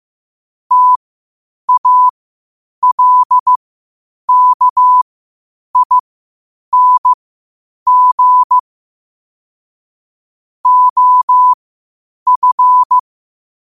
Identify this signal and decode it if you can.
Morse code, heard as a keyed tone, about 15 wpm, spaced as for 8 wpm, 1000 Hz. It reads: TALKING OF
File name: qotdfiles / 8wpm